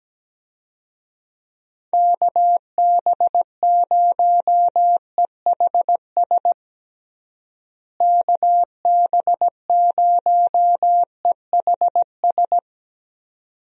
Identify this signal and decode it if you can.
Morse code, heard as a keyed tone, about 17 wpm, 700 Hz. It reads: KB0EHS KB0EHS
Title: USA Call Signs - 6 Characters - Repeated 2x - 17wpm